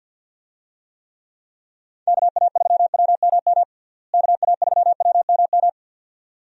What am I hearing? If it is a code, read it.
Morse code, heard as a keyed tone, about 53 wpm, 700 Hz. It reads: KA3WMM KA3WMM
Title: USA Call Signs - 6 Characters - Repeated 2x - 50wpm